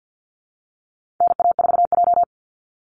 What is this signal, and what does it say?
Morse code, heard as a keyed tone, about 50 wpm, 700 Hz. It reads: NA4J
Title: Call Signs Repeated 2x - 50wpm